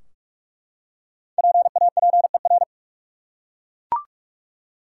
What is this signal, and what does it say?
Morse code, heard as a keyed tone, about 43 wpm, 700 Hz. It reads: PAPER
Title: Five Letter Words - 45wpm